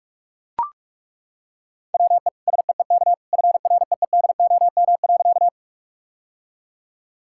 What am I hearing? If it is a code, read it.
Morse code, heard as a keyed tone, about 41 wpm, 700 Hz. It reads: WE SEEK FREEDOM.
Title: Sentences from Top 1600 Words 45wpm